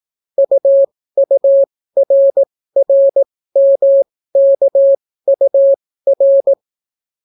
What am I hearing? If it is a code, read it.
Morse code, heard as a keyed tone, about 18 wpm, spaced as for 14 wpm, 550 Hz. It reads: UURRMKUR